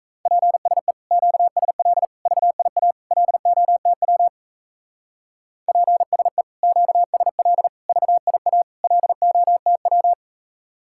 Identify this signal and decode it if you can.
Morse code, heard as a keyed tone, about 35 wpm, 700 Hz. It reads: PSE QSL VIA LOTW PSE QSL VIA LOTW
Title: Speed-Racing Common QSO Phrases - 28wpm